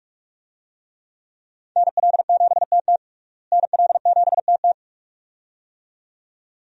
Morse code, heard as a keyed tone, about 46 wpm, 700 Hz. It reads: NL7TT NL7TT
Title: USA Call Signs - 5 Characters - Repeated 2x - 45wpm